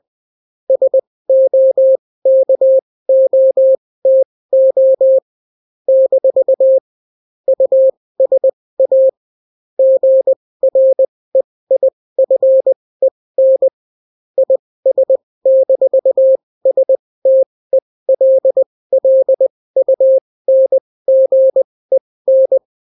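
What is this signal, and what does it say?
Morse code, heard as a keyed tone, about 20 wpm, spaced as for 16 wpm, 530 Hz. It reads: SOKOTO - USA GREIFEN IS-STELLUNGEN
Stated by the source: Daily Morse weather and top stories at 16 wpm, clean without QRN.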